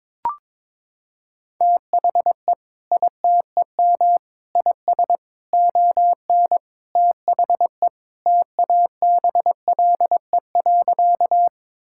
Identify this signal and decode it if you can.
Morse code, heard as a keyed tone, about 22 wpm, 700 Hz. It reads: THE ITEM IS ON THE TABLE.
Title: Sentences from Top 1000 Words - Review Part 6 - 22wpm